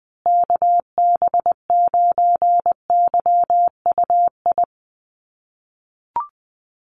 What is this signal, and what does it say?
Morse code, heard as a keyed tone, about 19 wpm, 700 Hz. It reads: KB9YUI
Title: USA Call Signs Repeated 3x 20wpm